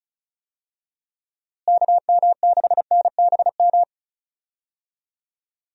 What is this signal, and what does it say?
Morse code, heard as a keyed tone, about 34 wpm, 700 Hz. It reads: KM6NBM
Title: USA Call Signs - 6 Characters - Repeated 2x - 35wpm